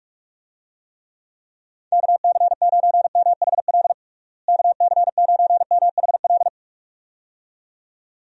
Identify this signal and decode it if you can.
Morse code, heard as a keyed tone, about 46 wpm, 700 Hz. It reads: KC9MHL KC9MHL
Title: USA Call Signs - 6 Characters - Repeated 2x - 45wpm